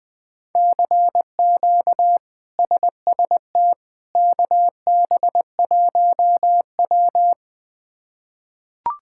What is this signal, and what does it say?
Morse code, heard as a keyed tone, about 21 wpm, 700 Hz. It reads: CQ SST KB1W
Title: K1USN SST QSO Practice - 20wpm